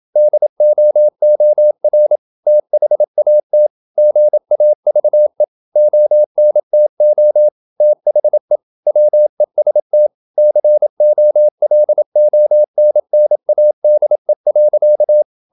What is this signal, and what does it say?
Morse code, heard as a keyed tone, about 27 wpm, 600 Hz. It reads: DOOR THAT GAVE ONTO THE WEST COLONNADE.
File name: Never 0027wpm